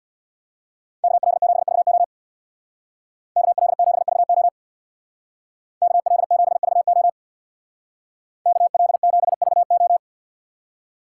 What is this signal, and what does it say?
Morse code, heard as a keyed tone, about 59 wpm, 700 Hz. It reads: KL7VO KL7VO KL7VO KL7VO
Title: Speed Racing - USA 2x2 Call Signs 50wpm